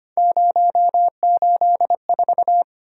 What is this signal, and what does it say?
Morse code, heard as a keyed tone, about 25 wpm, 700 Hz. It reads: 084
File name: Three Letter-Number Combinations - 25wpm